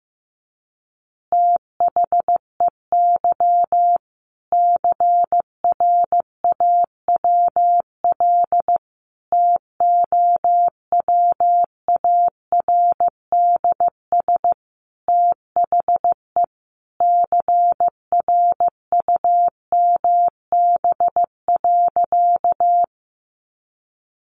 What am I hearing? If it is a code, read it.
Morse code, heard as a keyed tone, about 15 wpm, 700 Hz. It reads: THEY CRAWL TOWARDS THE CRUMB.
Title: Easy Phrases 15wpm